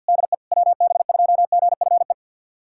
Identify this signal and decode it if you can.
Morse code, heard as a keyed tone, about 49 wpm, 700 Hz. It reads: DE WB2GFE
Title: USA Callsign DE Callsign 50wpm